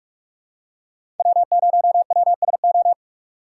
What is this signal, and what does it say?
Morse code, heard as a keyed tone, about 46 wpm, 700 Hz. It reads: W0WSO
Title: USA Call Signs - 5 Characters - 3x Character Spacing - 45wpm